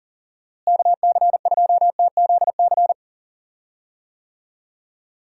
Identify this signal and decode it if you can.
Morse code, heard as a keyed tone, about 40 wpm, 700 Hz. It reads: KC2TZC
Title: USA Call Signs - 6 Characters - Repeated 2x - 40wpm